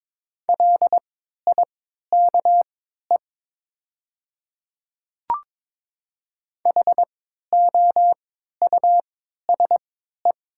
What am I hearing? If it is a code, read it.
Morse code, heard as a keyed tone, about 22 wpm, spaced as for 12 wpm, 700 Hz. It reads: LIKE HOUSE
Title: 3x Character Spacing - Top 200 Words - 22wpm